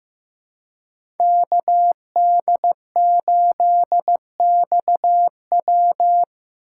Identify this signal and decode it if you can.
Morse code, heard as a keyed tone, about 15 wpm, 700 Hz. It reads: KD8XW
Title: USA Call Signs - 5 Characters - 6x Character Spacing - 15wpm